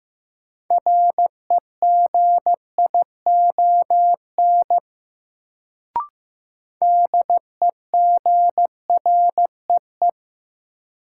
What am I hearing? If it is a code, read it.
Morse code, heard as a keyed tone, about 15 wpm, 700 Hz. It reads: REGION DEGREE
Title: Sentences from Top 1500 Words - New Word Drill Alt - 15wpm